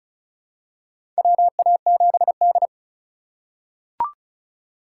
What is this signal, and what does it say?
Morse code, heard as a keyed tone, about 35 wpm, 700 Hz. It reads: WA7D
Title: USA Call Signs - 4 Characters - 2x Character Spacing - 35wpm